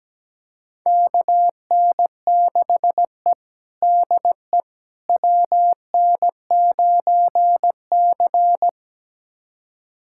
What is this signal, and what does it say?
Morse code, heard as a keyed tone, about 17 wpm, 700 Hz. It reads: KN6E DE WN9C
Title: USA Callsign DE Callsign - 4 Characters - 17wpm